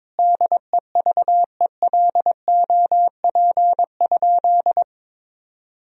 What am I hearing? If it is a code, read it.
Morse code, heard as a keyed tone, about 21 wpm, 700 Hz. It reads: DEVELOP?
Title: Sentences from Top 1000 Words - Review Part 2 - 22wpm